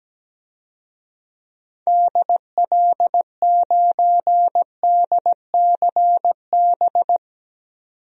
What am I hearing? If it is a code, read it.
Morse code, heard as a keyed tone, about 17 wpm, 700 Hz. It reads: DL9DCB
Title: DXCC-100 Call Signs - 6 Characters - 17wpm